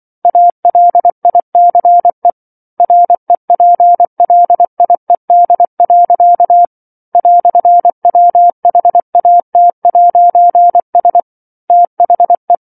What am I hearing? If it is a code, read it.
Morse code, heard as a keyed tone, about 24 wpm, 700 Hz. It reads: ALICE REPLIED. "WHAT'S THE
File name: 07-Chapter-07 24 WPM